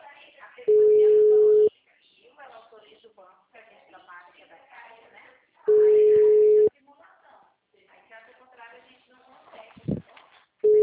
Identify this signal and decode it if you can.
Ringback tone (Europe, 425 Hz), with speech.